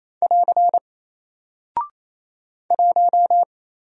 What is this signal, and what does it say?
Morse code, heard as a keyed tone, about 27 wpm, 700 Hz. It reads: <AR> 1
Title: Single Character (Letter:Number:Punctuation:Prosign) 28wpm